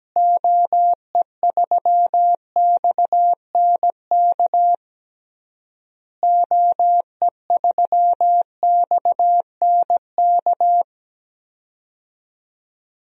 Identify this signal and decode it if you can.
Morse code, heard as a keyed tone, about 17 wpm, 700 Hz. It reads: OE3XNK OE3XNK
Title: DXCC-100 Call Signs - 6 Characters - Repeated 3x - 17wpm